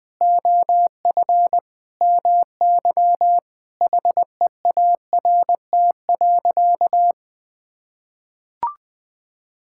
Morse code, heard as a keyed tone, about 20 wpm, 700 Hz. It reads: OF MY HEART.
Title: Sentences with 6-Word Phrases - Spoken Beginning - 20wpm